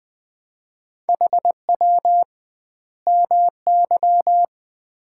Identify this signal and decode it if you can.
Morse code, heard as a keyed tone, about 20 wpm, 700 Hz. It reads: HW MY
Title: Common QSO Phrases - 2x Word-Spacing - 20wpm